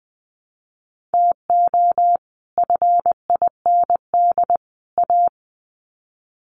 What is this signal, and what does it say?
Morse code, heard as a keyed tone, about 20 wpm, 700 Hz. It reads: TO FIND A
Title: Sets of 3 Words 20wpm